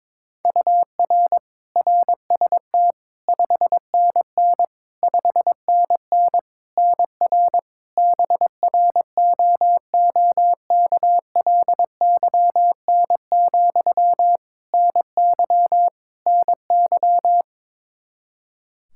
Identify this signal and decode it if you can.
Morse code, heard as a keyed tone, about 21 wpm, 700 Hz. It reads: UR RST 5NN 5NN NR BROOKLYN, NY NY
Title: K1USN SST QSO Practice 22wpm